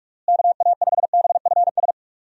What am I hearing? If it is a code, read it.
Morse code, heard as a keyed tone, about 46 wpm, 700 Hz. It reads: KA5BFS
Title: USA Call Signs - 6 Characters - Repeated 2x - 45wpm